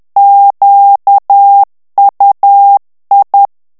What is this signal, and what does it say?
Morse code, heard as a keyed tone, about 10 wpm, 800 Hz. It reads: QUI
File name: FRANCAIS TEXTE CLAIR 2 VITESSE 10 MOTS MINUTES